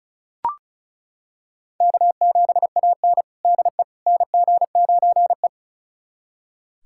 Morse code, heard as a keyed tone, about 36 wpm, 700 Hz. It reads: K7AN DE NG9E
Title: USA Callsign DE Callsign - 4 Characters - 35wpm